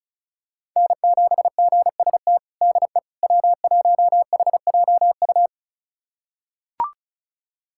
Morse code, heard as a keyed tone, about 36 wpm, 700 Hz. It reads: N7GST DE W1HJU
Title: USA Callsign DE Callsign - 5 Characters - 35wpm